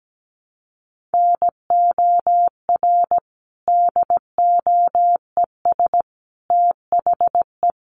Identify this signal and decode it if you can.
Morse code, heard as a keyed tone, about 17 wpm, 700 Hz. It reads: NOR DOES THE
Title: Sentences from Top 1500 Words 17wpm